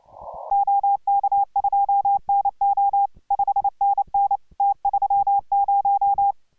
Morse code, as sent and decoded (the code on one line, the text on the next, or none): --- -.- ..--- -. --- / ..... -. -. / - ...-- -----
OK2NO 5NN T30